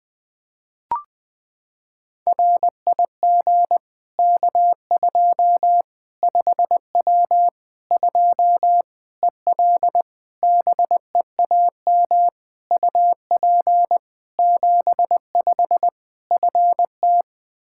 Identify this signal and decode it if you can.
Morse code, heard as a keyed tone, about 20 wpm, 700 Hz. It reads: RIG K2 5W 2 EL BEAM UP 75 FT
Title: Practice QSO - Exchange 2 - Info - 20wpm